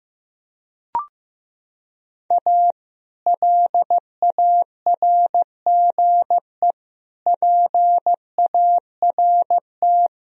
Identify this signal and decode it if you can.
Morse code, heard as a keyed tone, about 15 wpm, 700 Hz. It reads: A LARGE PART
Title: Sentences from Top 1000 Words - Review Part 1 - 15wpm